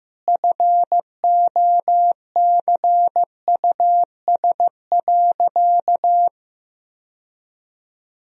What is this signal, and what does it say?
Morse code, heard as a keyed tone, about 15 wpm, 700 Hz. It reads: FOCUS.
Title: Sentences from Top 1000 Words - Review Part 4 - 15wpm